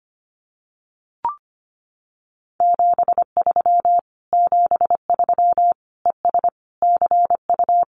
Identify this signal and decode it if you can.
Morse code, heard as a keyed tone, about 25 wpm, 700 Hz. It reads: 73 73 ES CU
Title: Common QSO Phrases Encore 25wpm